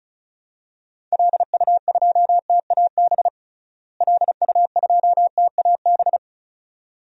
Morse code, heard as a keyed tone, about 36 wpm, 700 Hz. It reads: LU2TAB LU2TAB
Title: DXCC-100 Call Signs - 6 Characters - Repeated 2x - 35wpm